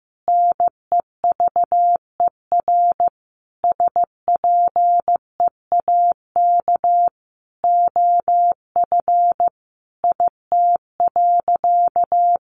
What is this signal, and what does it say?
Morse code, heard as a keyed tone, about 15 wpm, 700 Hz. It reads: NEVER SPEAK OF IT.
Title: Sentences from Top 1500 Words 15wpm